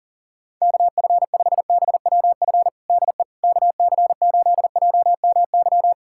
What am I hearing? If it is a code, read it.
Morse code, heard as a keyed tone, about 40 wpm, 700 Hz. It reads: KF5BWF DE KC8JMY
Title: USA Callsign DE Callsign - 6 Characters - 40wpm